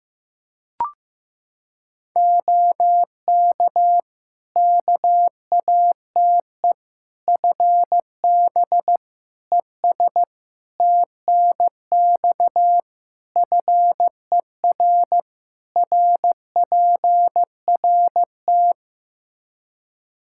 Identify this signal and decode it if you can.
Morse code, heard as a keyed tone, about 15 wpm, 700 Hz. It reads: OK KATE FB ES TNX FER RPRT
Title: Practice QSO - Exchange 2 - Info - 15wpm